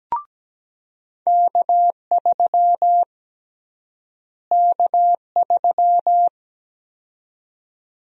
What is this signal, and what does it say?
Morse code, heard as a keyed tone, about 17 wpm, 700 Hz. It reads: K3 K3
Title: USA Call Sign Prefixes - Repeated 2x - 17wpm